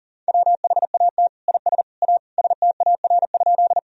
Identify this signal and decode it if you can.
Morse code, heard as a keyed tone, about 40 wpm, 700 Hz. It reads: WHAT IS A STAR?
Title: Sentences from Top 1000 Words - Review Part 2 - 40wpm